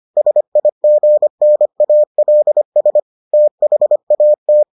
Morse code, heard as a keyed tone, about 25 wpm, 600 Hz. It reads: SIGNALS THAT